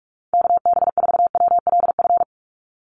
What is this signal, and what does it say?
Morse code, heard as a keyed tone, about 44 wpm, 700 Hz. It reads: KB4WLF
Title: Call Signs - 45wpm